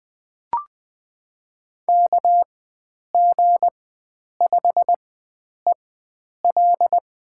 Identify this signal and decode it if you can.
Morse code, heard as a keyed tone, about 20 wpm, spaced as for 9 wpm, 700 Hz. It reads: KG5EL
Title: USA Call Signs - 4x Character Spacing - 20wpm